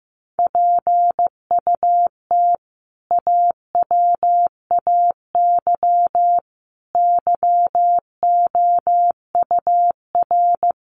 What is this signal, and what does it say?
Morse code, heard as a keyed tone, about 15 wpm, 700 Hz. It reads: PUT AWAY YOUR
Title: Sentences from Top 1400 Words 15wpm